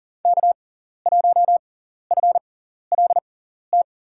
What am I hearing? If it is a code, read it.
Morse code, heard as a keyed tone, about 40 wpm, spaced as for 14 wpm, 700 Hz. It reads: K1FLT
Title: USA Call Signs - 5 Characters - 6x Character Spacing - 40wpm